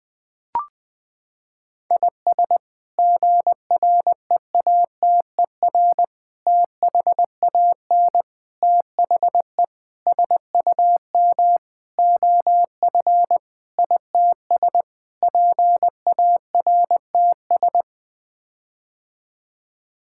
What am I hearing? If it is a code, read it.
Morse code, heard as a keyed tone, about 20 wpm, 700 Hz. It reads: IS GREATER THAN THE SUM OF ITS PARTS
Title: Sets of 8 Words - 20wpm